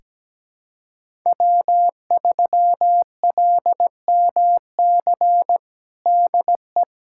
Morse code, heard as a keyed tone, about 17 wpm, 700 Hz. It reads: W3LMC DE
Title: USA Callsign DE Callsign - 5 Characters - 17wpm